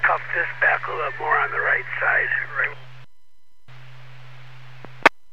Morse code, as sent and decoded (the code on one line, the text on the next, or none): none